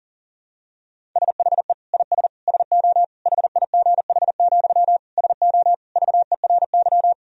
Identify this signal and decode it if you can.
Morse code, heard as a keyed tone, about 40 wpm, 700 Hz. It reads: SHE IS SO HIGH, SO VERY
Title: Sentences from Top 1000 Words - Review Part 1 - 40wpm